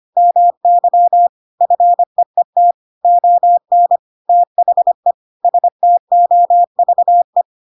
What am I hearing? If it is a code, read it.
Morse code, heard as a keyed tone, about 25 wpm, 700 Hz. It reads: MY FEET ON THE STOVE
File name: Pinocchio / Pinocchio ch7 25 25 ik1hge